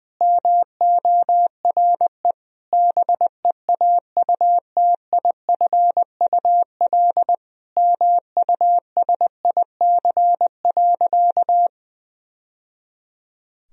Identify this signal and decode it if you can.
Morse code, heard as a keyed tone, about 21 wpm, 700 Hz. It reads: MORE BEAUTIFUL MUSIC.
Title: Sentences from Top 1000 Words - Review Part 1 - 20wpm